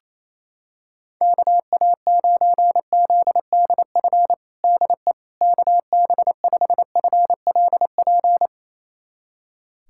Morse code, heard as a keyed tone, about 28 wpm, 700 Hz. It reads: KA9ZDF DE KB5FLP
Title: USA Callsign DE Callsign - 6 Characters - 28wpm